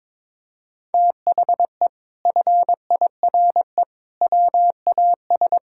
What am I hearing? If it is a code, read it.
Morse code, heard as a keyed tone, about 21 wpm, 700 Hz. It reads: THE FIRE WAS
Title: Sentences from Top 1000 Words - Review Part 6 - 22wpm